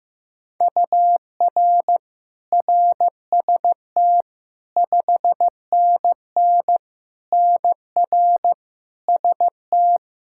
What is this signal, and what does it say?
Morse code, heard as a keyed tone, about 15 wpm, 700 Hz. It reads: UR RST 5NN NR ST
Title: Practice QSO - Exchange 1 - Beginning - 15wpm